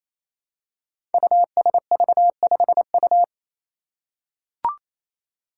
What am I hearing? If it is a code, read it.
Morse code, heard as a keyed tone, about 27 wpm, 700 Hz. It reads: USV5U
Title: Five Challenging Letter-Number Combinations 28wpm